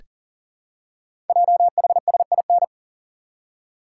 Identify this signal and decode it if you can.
Morse code, heard as a keyed tone, about 39 wpm, 700 Hz. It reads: JHSIN
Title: Code Groups - 5 Letters - 40wpm